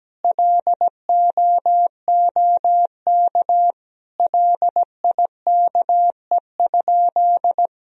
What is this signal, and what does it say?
Morse code, heard as a keyed tone, about 17 wpm, 700 Hz. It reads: LOOK LIKE?
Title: Sentences from Top 100 Words and Abbreviations 17wpm